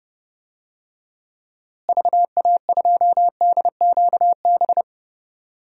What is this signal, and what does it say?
Morse code, heard as a keyed tone, about 30 wpm, 700 Hz. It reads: VA2DQB
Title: DXCC-100 Call Signs - 6 Characters - Repeated 3x - 30wpm